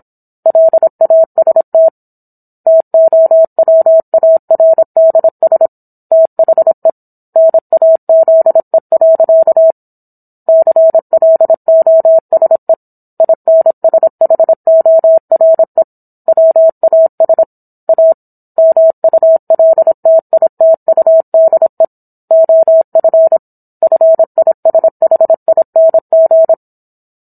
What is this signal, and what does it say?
Morse code, heard as a keyed tone, about 27 wpm, 650 Hz. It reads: LAST TOWARDS THE NAZE. CLOSE INSHORE WAS A MULTITUDE OF FISHING